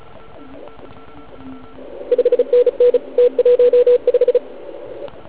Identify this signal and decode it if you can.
Morse code, heard as a keyed tone, about 34 wpm, 490 Hz, with radio noise. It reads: 5NN T15
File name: Archiv / CUC149 / OK1IF CUC149